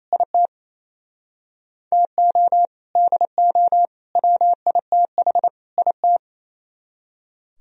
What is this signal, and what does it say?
Morse code, heard as a keyed tone, about 33 wpm, 700 Hz. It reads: IT TO DO WITH IT
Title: Speed-Racing - Sets of 4 Words 28wpm